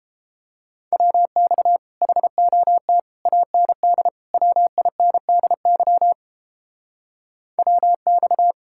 Morse code, heard as a keyed tone, about 31 wpm, 700 Hz. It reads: WX HOT AND WINDY WX